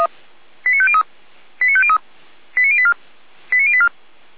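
Five-tone selective call (ZVEI), heard as quick runs of five tones, tones of about 70 ms. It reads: 89503 89503 89084 89084
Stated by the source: Five-tone ZVEI selective call 89120 received by SDR.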